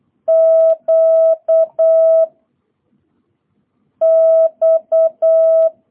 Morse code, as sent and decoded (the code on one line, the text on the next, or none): --.- / -..-
Q X